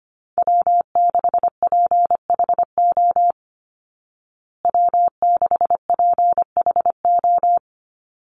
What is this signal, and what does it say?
Morse code, heard as a keyed tone, about 25 wpm, 700 Hz. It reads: W6PHO W6PHO
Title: Call Signs Repeated 3x - 25wpm